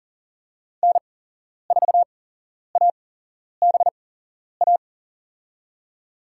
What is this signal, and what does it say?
Morse code, heard as a keyed tone, about 40 wpm, spaced as for 11 wpm, 700 Hz. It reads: N4ABA
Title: USA Call Signs - 8x Character Spacing - 40wpm